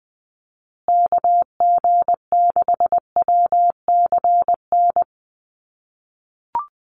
Morse code, heard as a keyed tone, about 20 wpm, 700 Hz. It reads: KG6WCN
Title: USA Call Signs Repeated 3x 20wpm